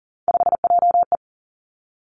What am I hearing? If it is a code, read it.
Morse code, heard as a keyed tone, about 40 wpm, 690 Hz. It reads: 5JE
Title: Three Challenging Letter-Number Combinations 40wpm